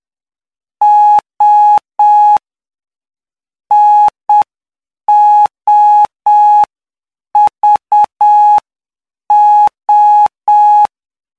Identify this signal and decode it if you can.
Morse code, heard as a keyed tone, about 7 wpm, spaced as for 6 wpm, 840 Hz. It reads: O NOVO
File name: cw / py2lcd / 15-PY2LCD-Li��o 24